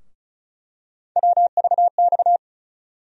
Morse code, heard as a keyed tone, about 34 wpm, 700 Hz. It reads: WVX
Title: USA Call Sign Postfixes - 3 Letters 35wpm